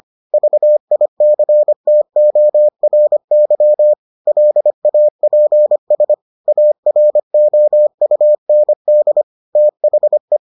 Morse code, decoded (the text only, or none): VICTORY LAPS AROUND THE